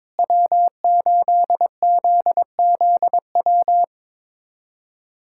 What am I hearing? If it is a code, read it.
Morse code, heard as a keyed tone, about 22 wpm, 700 Hz. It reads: W8ZZW
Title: USA Callsign DE Callsign 22wpm